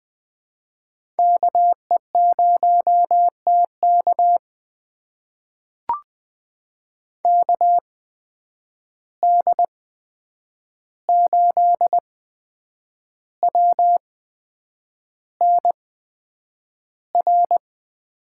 Morse code, heard as a keyed tone, about 20 wpm, spaced as for 8 wpm, 700 Hz. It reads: KE0TK KD8WNR
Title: USA Call Signs - 8x Character Spacing - 20wpm